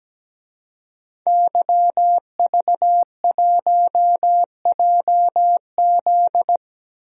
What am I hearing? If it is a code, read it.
Morse code, heard as a keyed tone, about 17 wpm, 700 Hz. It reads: YV1JZ
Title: DXCC-100 Call Signs - 5 Characters - Repeated 3x - 17wpm